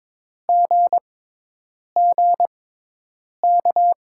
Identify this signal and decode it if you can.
Morse code, heard as a keyed tone, about 21 wpm, spaced as for 8 wpm, 700 Hz. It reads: GGK
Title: USA Call Signs - 6 Characters - 6x Character Spacing - 22wpm